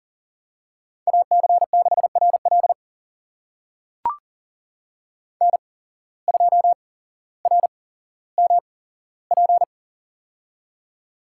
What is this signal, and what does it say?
Morse code, heard as a keyed tone, about 43 wpm, spaced as for 17 wpm, 700 Hz. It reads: AC6RL N2RMP
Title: USA Call Signs - 5 Characters - 8x Character Spacing - 40wpm